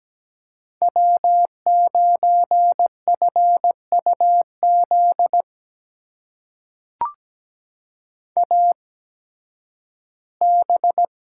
Morse code, heard as a keyed tone, about 17 wpm, spaced as for 11 wpm, 700 Hz. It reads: W9FUZ AB
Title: USA Call Signs - 5 Characters - 8x Character Spacing - 17wpm